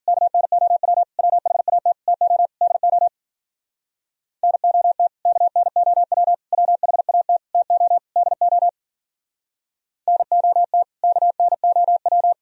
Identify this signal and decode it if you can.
Morse code, heard as a keyed tone, about 51 wpm, 700 Hz. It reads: KNOW WHAT TO DO NOT KNOW WHAT TO DO NOT KNOW